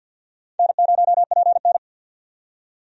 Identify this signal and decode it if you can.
Morse code, heard as a keyed tone, about 50 wpm, 700 Hz. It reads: N0PN
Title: USA Call Signs - 4 Characters - 50wpm